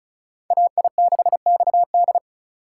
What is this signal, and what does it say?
Morse code, heard as a keyed tone, about 34 wpm, 700 Hz. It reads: AI6XD
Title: USA Call Signs - 5 Characters - 4x Character Spacing - 35wpm